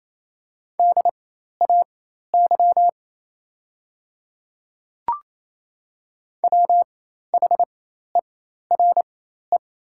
Morse code, heard as a keyed tone, about 28 wpm, spaced as for 13 wpm, 700 Hz. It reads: DAY WHERE
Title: 4x Character Spacing - Top 200 Words - 28wpm